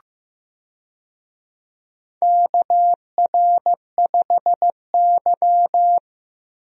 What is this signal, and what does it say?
Morse code, heard as a keyed tone, about 15 wpm, 700 Hz. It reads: KR5Y
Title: USA Call Signs - 4 Characters - Repeated 2x - 15wpm